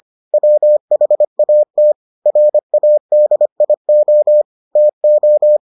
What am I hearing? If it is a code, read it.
Morse code, heard as a keyed tone, about 25 wpm, 600 Hz. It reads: WHAT RADIO TO